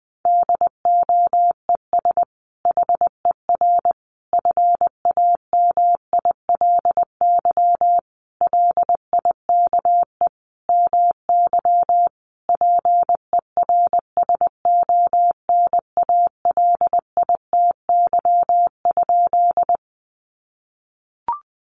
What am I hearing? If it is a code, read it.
Morse code, heard as a keyed tone, about 19 wpm, 700 Hz. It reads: DOES HER FAMILY LIKE MY PERSONALITY?
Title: Sentences from Top 1400 Words 20wpm